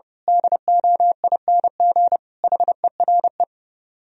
Morse code, heard as a keyed tone, about 28 wpm, 700 Hz. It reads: DOING HERE
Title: Speed-Racing - Sets of 5 Words 30wpm